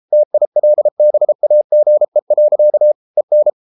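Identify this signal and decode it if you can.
Morse code, heard as a keyed tone, about 33 wpm, 600 Hz. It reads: TILBAGE. EN
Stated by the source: Morse code at 33 wpm, QRQ.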